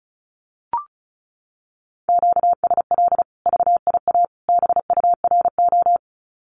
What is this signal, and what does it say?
Morse code, heard as a keyed tone, about 34 wpm, 700 Hz. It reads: QSL VIA BURO
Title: Common QSO Phrases 35wpm